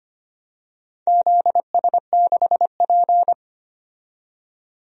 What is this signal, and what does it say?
Morse code, heard as a keyed tone, about 25 wpm, 700 Hz. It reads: ZS6P
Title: DXCC-100 Call Signs - 4 Characters - 25wpm